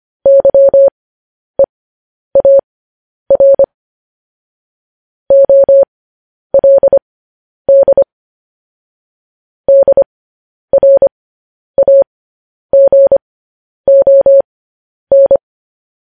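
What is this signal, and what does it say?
Morse code, heard as a keyed tone, about 25 wpm, spaced as for 10 wpm, 550 Hz. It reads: YEAR OLD DRAGON
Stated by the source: Morse code news headlines practice transmission.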